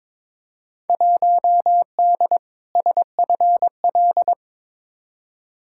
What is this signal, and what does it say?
Morse code, heard as a keyed tone, about 21 wpm, 700 Hz. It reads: 1D SFL
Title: Field Day Exchanges - 22wpm